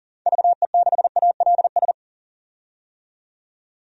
Morse code, heard as a keyed tone, about 38 wpm, 700 Hz. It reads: VE6ALS
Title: DXCC-100 Call Signs - 6 Characters - 40wpm